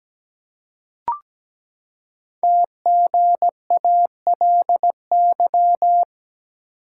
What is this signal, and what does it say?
Morse code, heard as a keyed tone, about 17 wpm, 700 Hz. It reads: TGALY
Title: Code Groups - 5 Letters - 17wpm